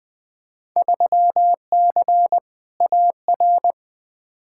Morse code, heard as a keyed tone, about 20 wpm, 700 Hz. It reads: 3C AR
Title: Field Day Exchanges Encore - 20wpm